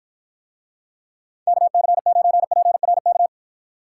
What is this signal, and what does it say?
Morse code, heard as a keyed tone, about 50 wpm, 700 Hz. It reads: KC9PRK